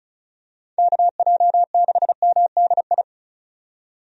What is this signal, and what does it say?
Morse code, heard as a keyed tone, about 36 wpm, 700 Hz. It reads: KJ6MDI
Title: USA Call Signs - 6 Characters - 35wpm